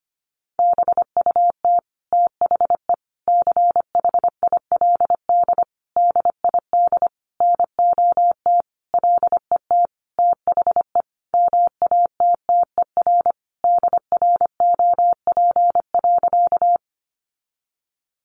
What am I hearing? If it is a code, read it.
Morse code, heard as a keyed tone, about 25 wpm, 700 Hz. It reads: BUT THE CHILD DID NOT LET THE MATTER DROP.
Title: Sentences from Top 1400 Words 25wpm